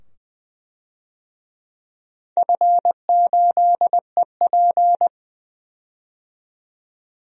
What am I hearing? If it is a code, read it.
Morse code, heard as a keyed tone, about 20 wpm, 700 Hz. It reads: F8EP
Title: DXCC-100 Call Signs - 4 Characters - 20wpm